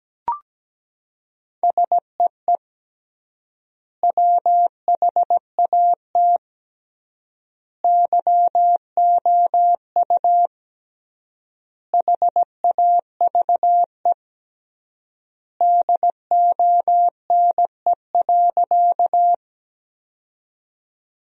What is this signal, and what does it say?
Morse code, heard as a keyed tone, about 17 wpm, 700 Hz. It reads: SEE WHAT YOU HAVE DONE.
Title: 3x Word-Spacing - Sentences from Top 500 Words - 17wpm